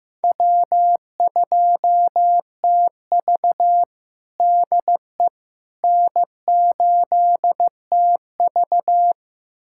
Morse code, heard as a keyed tone, about 15 wpm, 700 Hz. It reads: W2TV DE N8TV
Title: USA Callsign DE Callsign - 4 Characters - 15wpm